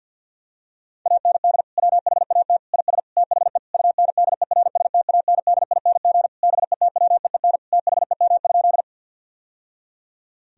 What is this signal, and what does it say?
Morse code, heard as a keyed tone, about 50 wpm, 700 Hz. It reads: AND WHAT IS THE UNDERSTANDING BETWEEN THEM?
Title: Sentences from Top 1000 Words - Review Part 3 - 50wpm